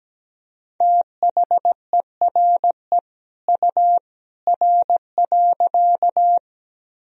Morse code, heard as a keyed tone, about 17 wpm, 700 Hz. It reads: THERE U R.
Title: Sentences from Top 100 Words and Abbreviations 17wpm